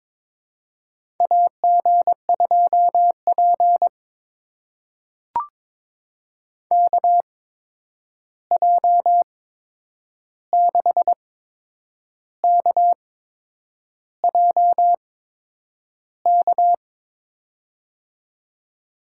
Morse code, heard as a keyed tone, about 23 wpm, spaced as for 8 wpm, 700 Hz. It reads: AG2P KJ6KJK
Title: USA Call Signs - 8x Character Spacing - 22wpm